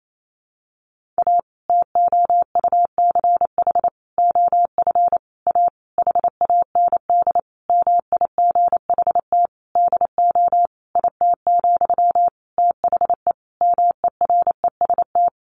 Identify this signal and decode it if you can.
Morse code, heard as a keyed tone, about 28 wpm, 700 Hz. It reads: A TOUCH OF A HAND MIGHT DO IT, THE MEREST
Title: A Sound Of Thunder 28wpm